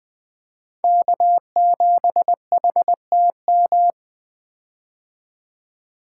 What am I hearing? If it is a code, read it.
Morse code, heard as a keyed tone, about 20 wpm, 700 Hz. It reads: K7HTM
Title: 2023 POTA Activator Call Signs - 20wpm